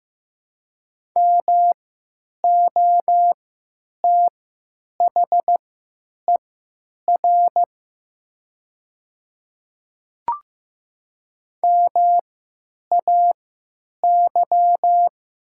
Morse code, heard as a keyed tone, about 15 wpm, spaced as for 9 wpm, 700 Hz. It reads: MOTHER MAY
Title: 3x Character Spacing - Top 200 Words - 15wpm